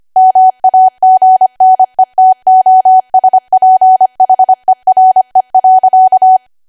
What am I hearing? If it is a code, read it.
Morse code, heard as a keyed tone, about 25 wpm, 750 Hz. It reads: MAGNETOSPHERE.